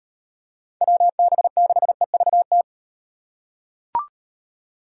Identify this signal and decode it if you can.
Morse code, heard as a keyed tone, about 38 wpm, 700 Hz. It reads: WB6EVT